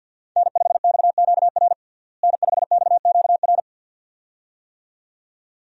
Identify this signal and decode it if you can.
Morse code, heard as a keyed tone, about 52 wpm, 700 Hz. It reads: N5XQR N5XQR
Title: USA Call Signs - 5 Characters - Repeated 2x - 50wpm